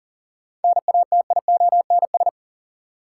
Morse code, heard as a keyed tone, about 40 wpm, 700 Hz. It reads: NATIONS
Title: Sets of 3 Words - New Word Drill - 40wpm